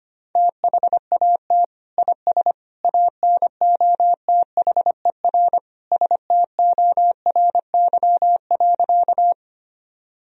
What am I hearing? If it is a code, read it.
Morse code, heard as a keyed tone, about 25 wpm, 700 Hz. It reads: THAT IS ANOTHER STORY.
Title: Sentences from Top 1000 Words - Review Part 1 - 25wpm